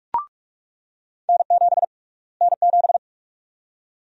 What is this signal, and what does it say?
Morse code, heard as a keyed tone, about 46 wpm, 700 Hz. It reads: N7 N7
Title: USA Call Sign Prefixes - Repeated 2x - 45wpm